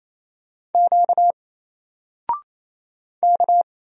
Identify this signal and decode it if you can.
Morse code, heard as a keyed tone, about 28 wpm, 700 Hz. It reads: Q K
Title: Single Letter-Number Alt 28wpm